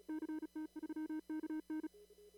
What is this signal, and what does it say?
Morse code, heard as a keyed tone, about 35 wpm, 310 Hz. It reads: CT3KN
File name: CW / kt5j wpx cw-op-k5pi